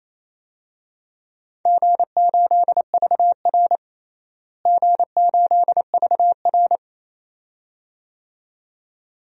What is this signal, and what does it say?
Morse code, heard as a keyed tone, about 28 wpm, 700 Hz. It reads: G8VR G8VR
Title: DXCC-100 Call Signs - 4 Characters - Repeated 2x - 28wpm